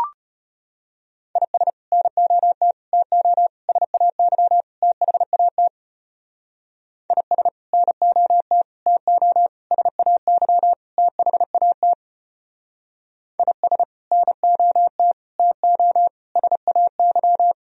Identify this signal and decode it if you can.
Morse code, heard as a keyed tone, about 31 wpm, 700 Hz. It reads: IS NOT TO SAY THAT IS NOT TO SAY THAT IS NOT TO SAY
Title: Speed-Racing - Sets of 5 Words 25wpm